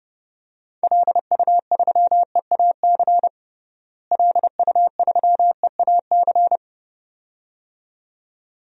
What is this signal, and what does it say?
Morse code, heard as a keyed tone, about 32 wpm, 700 Hz. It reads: LU3EAC LU3EAC
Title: DXCC-100 Call Signs - 6 Characters - Repeated 2x - 30wpm